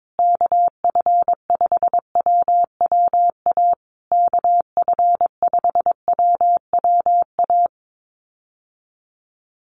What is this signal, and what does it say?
Morse code, heard as a keyed tone, about 22 wpm, 700 Hz. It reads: KF5WWA KF5WWA
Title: USA Call Signs Repeated 3x 22wpm